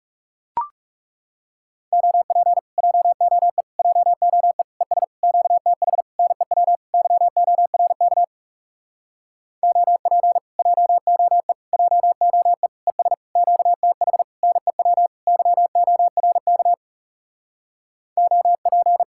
Morse code, heard as a keyed tone, about 40 wpm, 700 Hz. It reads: OP JOE JOE ES QTH NEW YORK OP JOE JOE ES QTH NEW YORK OP